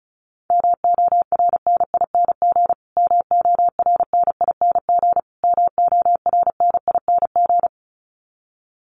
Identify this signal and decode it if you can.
Morse code, heard as a keyed tone, about 34 wpm, 700 Hz. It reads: MORNING MORNING MORNING
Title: Top 500 Words Repeated 35wpm